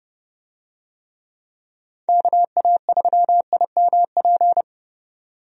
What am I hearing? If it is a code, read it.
Morse code, heard as a keyed tone, about 31 wpm, 700 Hz. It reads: KA3IMP
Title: USA Call Signs - 6 Characters - 30wpm